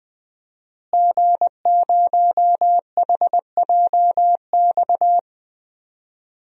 Morse code, heard as a keyed tone, about 20 wpm, 700 Hz. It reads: G0HJX
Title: DXCC-100 Call Signs - 5 Characters - Repeated 2x - 20wpm